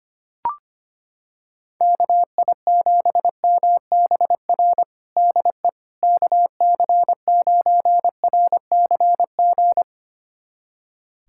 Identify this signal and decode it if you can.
Morse code, heard as a keyed tone, about 25 wpm, 700 Hz. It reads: KI7MBR DE KC9RCG
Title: USA Callsign DE Callsign - 6 Characters - 25wpm